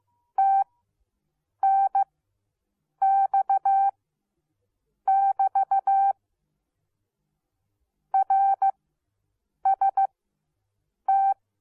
Morse code, heard as a keyed tone, about 15 wpm, spaced as for 7 wpm, 780 Hz. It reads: TNX <BT> RST